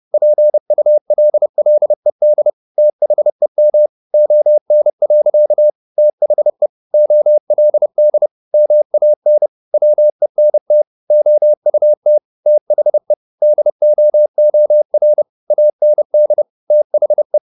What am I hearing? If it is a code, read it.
Morse code, heard as a keyed tone, about 30 wpm, 600 Hz. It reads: PULLED THEM ON. THE OLD MAN WENT OUT THE DOOR AND THE
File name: the old man and the sea / the old man and the sea 1 30wpm